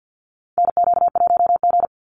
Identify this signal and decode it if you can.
Morse code, heard as a keyed tone, about 50 wpm, 700 Hz. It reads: NQ1G
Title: USA Call Signs 50wpm